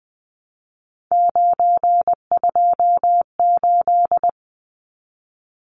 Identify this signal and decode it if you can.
Morse code, heard as a keyed tone, about 20 wpm, 700 Hz. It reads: 928
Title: Numbers - 3-Digit 20wpm